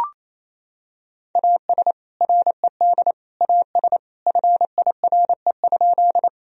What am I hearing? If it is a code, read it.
Morse code, heard as a keyed tone, about 28 wpm, 700 Hz. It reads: AS RED AS FIRE?
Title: Sentences from Top 1000 Words - Review Part 2 - 28wpm